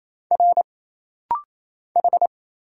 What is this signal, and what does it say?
Morse code, heard as a keyed tone, about 27 wpm, 700 Hz. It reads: R H
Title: Single Letters - Rapid-Fire 28wpm